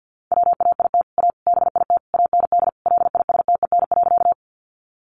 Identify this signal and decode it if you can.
Morse code, heard as a keyed tone, about 50 wpm, 700 Hz. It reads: WAIT A BIT AND LISTEN.
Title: Sentences from Top 900 Words 50wpm